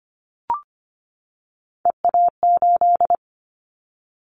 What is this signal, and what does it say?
Morse code, heard as a keyed tone, about 25 wpm, 700 Hz. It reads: EA8
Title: Callsign Prefixes for Top 100 DXCC - 25wpm